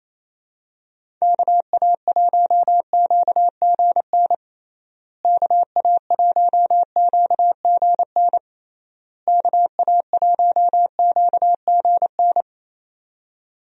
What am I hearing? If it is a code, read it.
Morse code, heard as a keyed tone, about 28 wpm, 700 Hz. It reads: KA1QGN KA1QGN KA1QGN
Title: USA Call Signs - 6 Characters - Repeated 3x - 28wpm